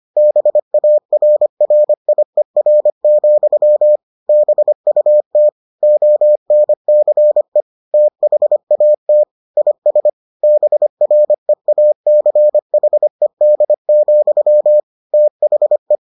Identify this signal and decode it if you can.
Morse code, heard as a keyed tone, about 25 wpm, 600 Hz. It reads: BARRIER, BUT ONCE THAT IS BREACHED, THE